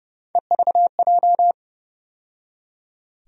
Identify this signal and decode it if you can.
Morse code, heard as a keyed tone, about 30 wpm, 700 Hz. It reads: EVJ
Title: USA Call Sign Postfixes - 3 Letters 30wpm